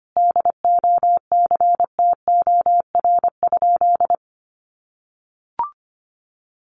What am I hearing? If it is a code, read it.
Morse code, heard as a keyed tone, about 25 wpm, 700 Hz. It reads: DOCTOR?
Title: Sentences from Top 1500 Words 25wpm